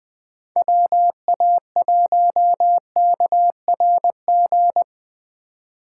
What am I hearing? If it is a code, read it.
Morse code, heard as a keyed tone, about 24 wpm, 700 Hz. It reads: WA1KRG
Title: Speed Racing - USA 2x3 Call Signs 20wpm